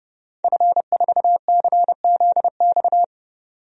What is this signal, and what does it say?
Morse code, heard as a keyed tone, about 29 wpm, 700 Hz. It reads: F4CZX
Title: DXCC-100 Call Signs - 5 Characters - 30wpm